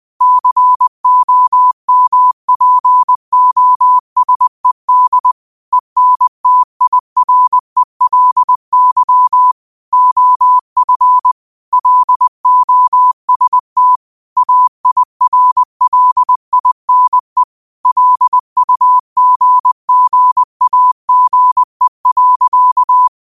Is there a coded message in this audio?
Yes: Morse code from 0.2 to 23.1 s, heard as a keyed tone, about 20 wpm, 1000 Hz. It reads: COMPOSED ENTIRELY OF LOST AIRLINE LUGGAGE.